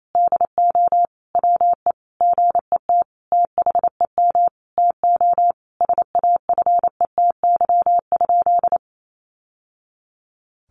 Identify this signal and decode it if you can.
Morse code, heard as a keyed tone, about 28 wpm, 700 Hz. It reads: DO WE GET THEM TO SAFETY?
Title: Sentences from Top 700 Words 28wpm